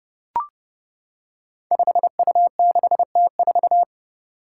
Morse code, heard as a keyed tone, about 30 wpm, 700 Hz. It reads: 5U6T4
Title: Five Challenging Letter-Number Combinations 30wpm